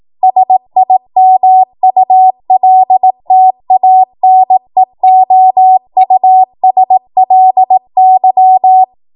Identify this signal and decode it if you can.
Morse code, heard as a keyed tone, about 18 wpm, 750 Hz. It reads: SIMULTANEOUSLY
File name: Morse / 18WPM